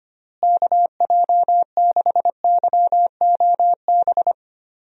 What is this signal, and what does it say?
Morse code, heard as a keyed tone, about 25 wpm, 700 Hz. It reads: KJ6YOB
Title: USA Call Signs - 6 Characters - Repeated 2x - 25wpm